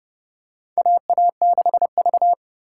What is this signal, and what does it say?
Morse code, heard as a keyed tone, about 32 wpm, 700 Hz. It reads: AA6V
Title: USA Call Signs - 4 Characters - 4x Character Spacing - 30wpm